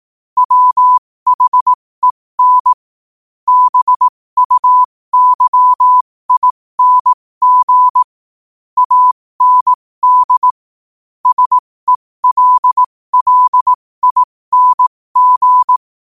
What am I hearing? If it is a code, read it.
Morse code, heard as a keyed tone, about 18 wpm, spaced as for 15 wpm, 1000 Hz. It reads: WHEN BUYING AND SELLING
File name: qotdfiles / 15wpm